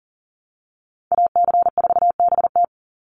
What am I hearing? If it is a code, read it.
Morse code, heard as a keyed tone, about 40 wpm, 700 Hz. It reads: AC4BT
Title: Call Signs - 40wpm